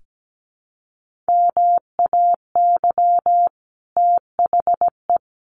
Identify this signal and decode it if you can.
Morse code, heard as a keyed tone, about 17 wpm, 700 Hz. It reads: MAY THE
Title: Sentences from Top 1500 Words 17wpm